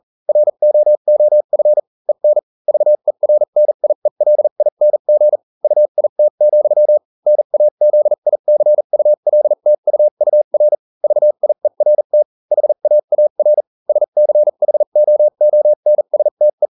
Morse code, heard as a keyed tone, about 40 wpm, spaced as for 35 wpm, 600 Hz. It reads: ROOF EN VERNIELING UIT, NAZICULTUUR VIERT HAAR SCHOONSTE